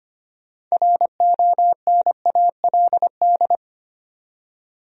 Morse code, heard as a keyed tone, about 25 wpm, 700 Hz. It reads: RONALD
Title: USA Common Names - 25wpm